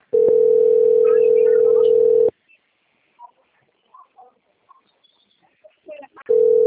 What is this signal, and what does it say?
Ringback tone (North America), 440 + 480 Hz.
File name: out- - - - - .94372